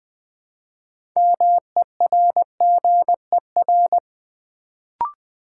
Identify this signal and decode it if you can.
Morse code, heard as a keyed tone, about 20 wpm, 700 Hz. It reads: MERGER
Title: Build A Word - 2 Letters At A Time - 6-Letter Words - 20wpm